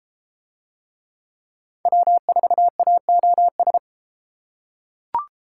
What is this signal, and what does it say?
Morse code, heard as a keyed tone, about 32 wpm, 700 Hz. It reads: W4AOS